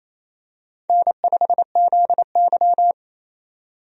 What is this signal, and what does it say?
Morse code, heard as a keyed tone, about 28 wpm, 700 Hz. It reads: N5ZY
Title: USA Call Signs - 4 Characters - 28wpm